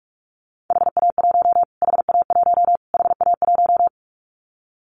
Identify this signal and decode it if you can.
Morse code, heard as a keyed tone, about 44 wpm, 700 Hz. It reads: HA1 HA1 HA1
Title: Callsign Prefixes for Top 25 DXCC - Repeated 3x - 45wpm